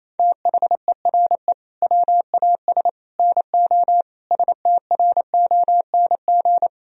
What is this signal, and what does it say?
Morse code, heard as a keyed tone, about 27 wpm, 700 Hz. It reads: THERE WAS NO STRONG
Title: Sentences from Top 1000 Words - Review Part 4 - 28wpm